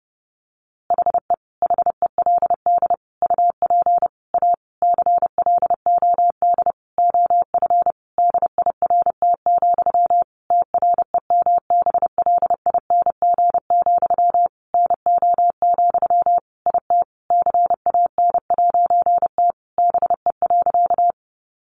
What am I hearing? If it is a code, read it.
Morse code, heard as a keyed tone, about 30 wpm, 700 Hz. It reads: HE HELD UP A CLOD OF DIRT, TREMBLING, NO, IT CAN'T BE.
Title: A Sound Of Thunder 30wpm